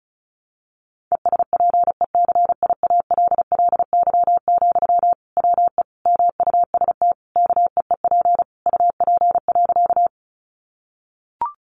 Morse code, heard as a keyed tone, about 34 wpm, 700 Hz. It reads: ESPECIALLY, WE MUST KEEP UP.
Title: Sentences from Top 700 Words 35wpm